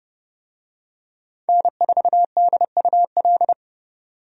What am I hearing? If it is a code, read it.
Morse code, heard as a keyed tone, about 32 wpm, 700 Hz. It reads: N4DUL
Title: USA Call Signs - 5 Characters - Repeated 3x - 30wpm